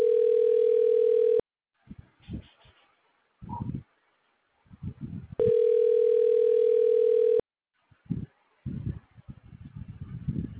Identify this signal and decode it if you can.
Ringback tone (North America), 440 + 480 Hz.